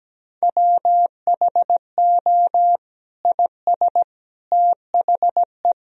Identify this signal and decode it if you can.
Morse code, heard as a keyed tone, about 17 wpm, 700 Hz. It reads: WHO IS THE
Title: Sentences from Top 1000 Words - Review Part 4 - 17wpm